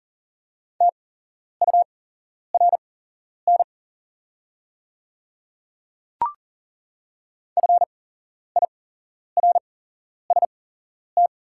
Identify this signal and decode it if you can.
Morse code, heard as a keyed tone, about 40 wpm, spaced as for 11 wpm, 700 Hz. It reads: TURN FIRST
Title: 8x Character Spacing - Top 500 Words - 40wpm